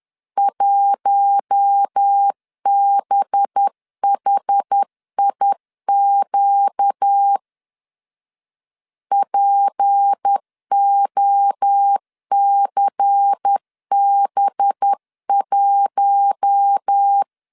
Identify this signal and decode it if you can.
Morse code, heard as a keyed tone, about 10 wpm, 800 Hz. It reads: 1BHIQ POCB1
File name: LEZIONI / 6 PKX0 / 6 P 10wpm 2432